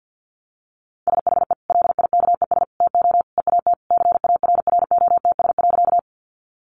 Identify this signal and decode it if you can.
Morse code, heard as a keyed tone, about 50 wpm, 700 Hz. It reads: SHE LIKES TO EAT CARROTS.
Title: Easy Phrases 50wpm